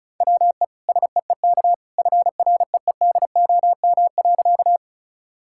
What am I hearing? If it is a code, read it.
Morse code, heard as a keyed tone, about 34 wpm, 700 Hz. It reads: WE SEEK FREEDOM.
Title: Sentences from Top 1600 Words 35wpm